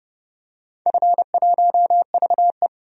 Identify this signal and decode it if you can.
Morse code, heard as a keyed tone, about 30 wpm, 700 Hz. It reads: F1VE
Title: DXCC-100 Call Signs - 4 Characters - 30wpm